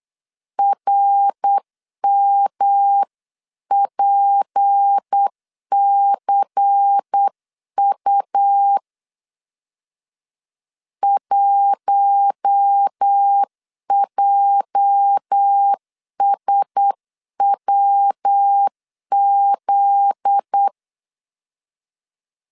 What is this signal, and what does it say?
Morse code, heard as a keyed tone, about 8 wpm, 800 Hz. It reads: RMPCU 1JSWZ
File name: CW / DET / 1 9 1500C S A 8wpm 5637